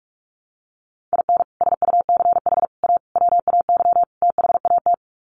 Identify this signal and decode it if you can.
Morse code, heard as a keyed tone, about 44 wpm, 700 Hz. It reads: IN SUCH A WAY THAT
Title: Sets of 5 Words 45wpm